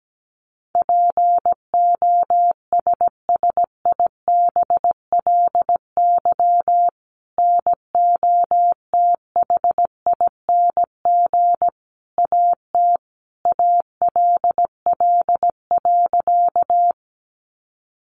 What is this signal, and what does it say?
Morse code, heard as a keyed tone, about 17 wpm, 700 Hz. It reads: POSSIBLY NOTHING AT ALL.
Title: Sentences from Top 1300 Words 17wpm